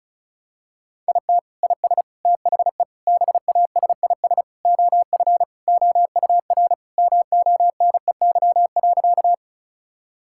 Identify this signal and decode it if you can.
Morse code, heard as a keyed tone, about 36 wpm, 700 Hz. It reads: IT IS THE BASIS OF OUR MONEY.
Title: Sentences from Top 1000 Words - Review Part 5 - 35wpm